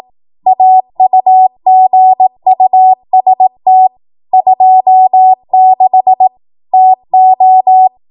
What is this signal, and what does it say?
Morse code, heard as a keyed tone, about 18 wpm, 750 Hz. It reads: AUGUST 26 TO
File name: Morse / 18WPM